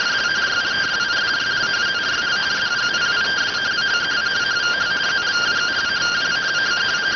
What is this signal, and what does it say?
Two-tone teleprinter signal, about 170 Hz shift.